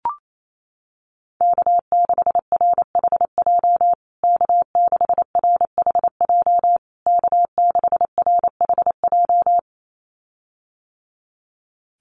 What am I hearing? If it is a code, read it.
Morse code, heard as a keyed tone, about 28 wpm, 700 Hz. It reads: K6RHJ K6RHJ K6RHJ
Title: USA Call Signs Repeated 3x 28wpm